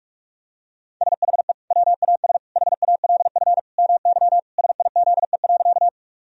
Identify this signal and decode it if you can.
Morse code, heard as a keyed tone, about 41 wpm, 700 Hz. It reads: SHE WAS HALF MY SIZE.
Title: Sentences from Top 1000 Words - Review Part 2 - 45wpm